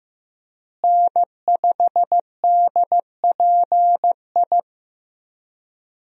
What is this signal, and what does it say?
Morse code, heard as a keyed tone, about 15 wpm, 700 Hz. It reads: N5DPI
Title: USA Call Signs - 5 Characters - Repeated 3x - 15wpm